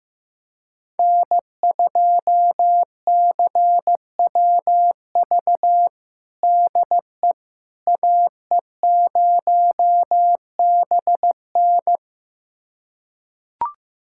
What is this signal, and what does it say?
Morse code, heard as a keyed tone, about 15 wpm, 700 Hz. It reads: N2CWV DE AE0BN
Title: USA Callsign DE Callsign - 5 Characters - 15wpm